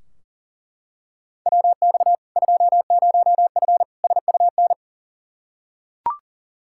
Morse code, heard as a keyed tone, about 40 wpm, 700 Hz. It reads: WX 20F SUN
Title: Practice QSO - Exchange 2 - Info - 40wpm